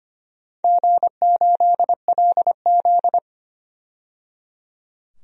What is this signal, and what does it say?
Morse code, heard as a keyed tone, about 25 wpm, 700 Hz. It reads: G8LZ
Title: DXCC-100 Call Signs - 4 Characters - 25wpm